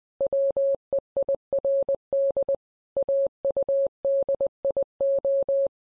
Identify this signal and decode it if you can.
Morse code, heard as a keyed tone, about 20 wpm, 550 Hz. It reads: WEIRD AUDIO